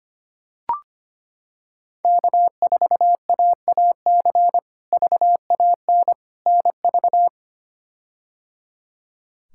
Morse code, heard as a keyed tone, about 25 wpm, 700 Hz. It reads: K4AAC VAN NV
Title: K1USN SST QSO Practice - 25wpm